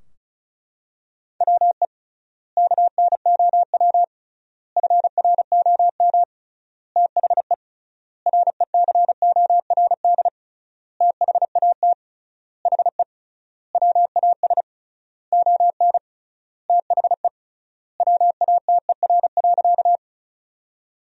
Morse code, heard as a keyed tone, about 34 wpm, 700 Hz. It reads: WE KNOW FROM THE RECORD THAT HE WAS ON THE WATER.
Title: 3x Word-Spacing - Sentences from Top 500 Words - 35wpm